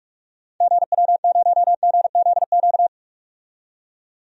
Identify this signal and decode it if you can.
Morse code, heard as a keyed tone, about 44 wpm, 700 Hz. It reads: GW0GZQ
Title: DXCC-100 Call Signs - 6 Characters - 45wpm